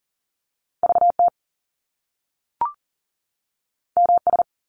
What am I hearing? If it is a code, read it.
Morse code, heard as a keyed tone, about 42 wpm, 700 Hz. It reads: VT MS
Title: US States 40wpm